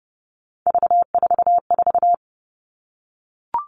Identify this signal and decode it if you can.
Morse code, heard as a keyed tone, about 31 wpm, 700 Hz. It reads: V44
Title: Three Challenging Letter-Number Combinations 30wpm